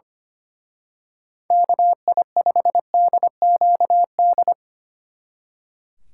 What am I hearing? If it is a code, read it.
Morse code, heard as a keyed tone, about 28 wpm, 700 Hz. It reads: KI5DQD
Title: Speed Racing - USA Call Signs 25wpm